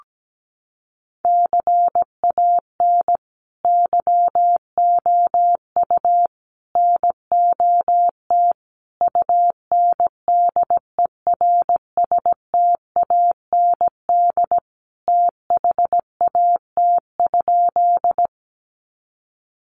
Morse code, heard as a keyed tone, about 17 wpm, 700 Hz. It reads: CAN YOU NOT UNDERSTAND THAT?
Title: Sentences from Top 600 Words 17wpm